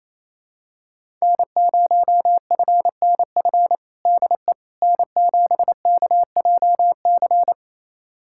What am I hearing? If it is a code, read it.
Morse code, heard as a keyed tone, about 28 wpm, 700 Hz. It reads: N0FNF DE N7KJC
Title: USA Callsign DE Callsign - 5 Characters - 28wpm